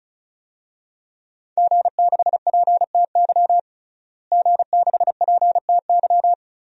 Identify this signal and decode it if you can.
Morse code, heard as a keyed tone, about 36 wpm, 700 Hz. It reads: G6PTY G6PTY
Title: DXCC-100 Call Signs - 5 Characters - Repeated 2x - 35wpm